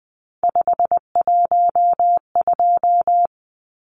Morse code, heard as a keyed tone, about 20 wpm, 700 Hz. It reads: 512
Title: Numbers - 3-Digit 20wpm